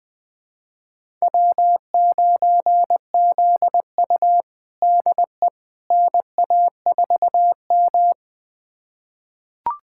Morse code, heard as a keyed tone, about 21 wpm, 700 Hz. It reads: W9ZU DE NA4M
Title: USA Callsign DE Callsign - 4 Characters - 20wpm